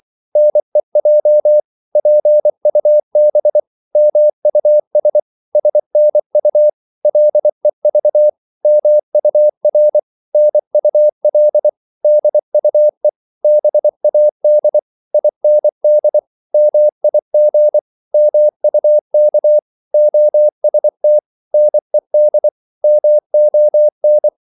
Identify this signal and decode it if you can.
Morse code, heard as a keyed tone, about 24 wpm, 600 Hz. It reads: NEJ PUB MUS SNU LEV MUR NUL DUE BAD IND MIG MUK OST NED MON
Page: Korte danske ord 24wpm